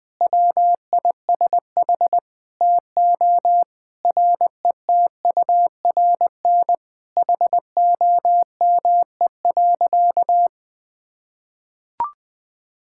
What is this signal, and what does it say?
Morse code, heard as a keyed tone, about 20 wpm, 700 Hz. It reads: WISH TO RETURN HOME.
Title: Sentences from Top 1600 Words 20wpm